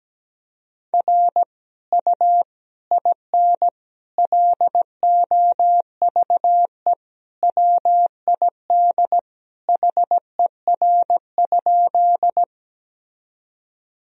Morse code, heard as a keyed tone, about 17 wpm, 700 Hz. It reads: R U IN LOVE WID HER?
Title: Sentences from Top 400 Words and Abbreviations 17wpm